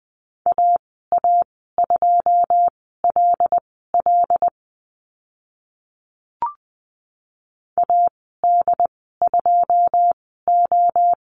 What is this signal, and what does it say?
Morse code, heard as a keyed tone, about 20 wpm, spaced as for 15 wpm, 700 Hz. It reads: AA2LL AD2O
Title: 082 Lesson 14g - Number 2 - Call Signs 20wpm